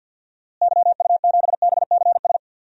Morse code, heard as a keyed tone, about 52 wpm, 700 Hz. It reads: YU7BCS
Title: DXCC-100 Call Signs - 6 Characters - Repeated 2x - 50wpm